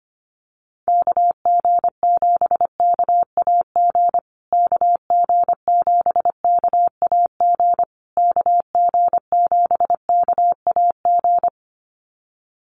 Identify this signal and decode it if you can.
Morse code, heard as a keyed tone, about 25 wpm, 700 Hz. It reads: KG7KAG KG7KAG KG7KAG
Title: USA Call Signs Repeated 3x 25wpm